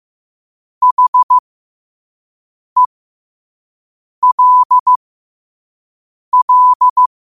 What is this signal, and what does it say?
Morse code, heard as a keyed tone, about 15 wpm, spaced as for 5 wpm, 1000 Hz. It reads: HELL
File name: qotdfiles / 5wpm